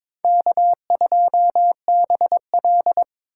Morse code, heard as a keyed tone, about 26 wpm, 700 Hz. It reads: K2BL
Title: Speed Racing - USA Call Signs 22wpm